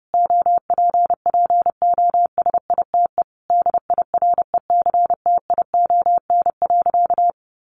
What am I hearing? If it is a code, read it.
Morse code, heard as a keyed tone, about 30 wpm, 700 Hz. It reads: OPPOSITE DIRECTION.
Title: Sentences from Top 1300 Words 30wpm